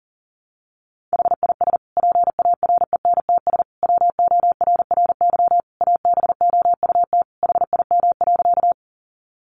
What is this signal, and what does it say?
Morse code, heard as a keyed tone, about 40 wpm, 700 Hz. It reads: HIS PARENTS WORRY ABOUT HIM.
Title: Easy Phrases 40wpm